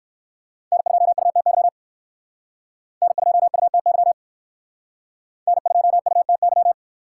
Morse code, heard as a keyed tone, about 60 wpm, 700 Hz. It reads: N2UTY N2UTY N2UTY